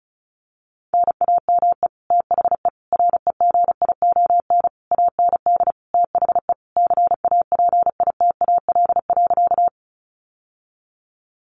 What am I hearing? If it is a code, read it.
Morse code, heard as a keyed tone, about 34 wpm, 700 Hz. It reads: NAME THE REGION AND THE CAPITAL.
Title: Sentences from Top 1500 Words 35wpm